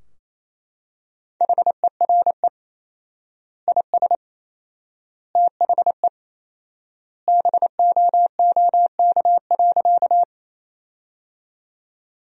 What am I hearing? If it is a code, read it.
Morse code, heard as a keyed tone, about 28 wpm, 700 Hz. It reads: HERE IS THE BOOK.
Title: 4x Word-Spacing - Sentences from Top 300 Words - 28wpm